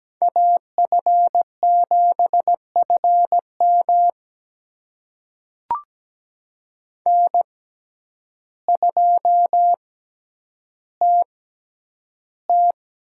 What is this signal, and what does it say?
Morse code, heard as a keyed tone, about 17 wpm, spaced as for 9 wpm, 700 Hz. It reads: AF7FM N2TT
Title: USA Call Signs - 5 Characters - 6x Character Spacing - 17wpm